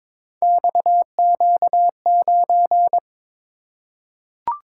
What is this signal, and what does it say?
Morse code, heard as a keyed tone, about 22 wpm, 700 Hz. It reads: XQ9
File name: practice / 22wpm / Three Letter-Number Combinations - 22wpm